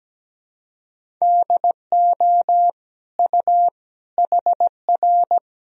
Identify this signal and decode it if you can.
Morse code, heard as a keyed tone, about 17 wpm, 700 Hz. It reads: DO U HR
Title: Sentences from Top 300 Words and Abbreviations 17wpm